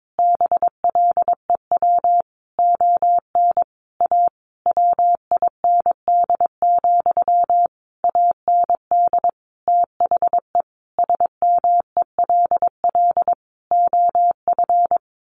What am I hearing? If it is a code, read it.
Morse code, heard as a keyed tone, about 22 wpm, 700 Hz. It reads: BLEW ON A WIND, AND THE SMELL OF
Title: A Sound Of Thunder 22wpm